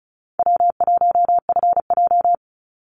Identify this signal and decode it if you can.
Morse code, heard as a keyed tone, about 34 wpm, 700 Hz. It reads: W1FJ
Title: Easy Call Signs Repeated 3x - 35wpm